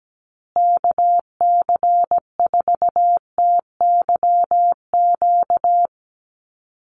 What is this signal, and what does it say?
Morse code, heard as a keyed tone, about 17 wpm, 700 Hz. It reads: KC4TYQ
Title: USA Call Signs Repeated 3x 17wpm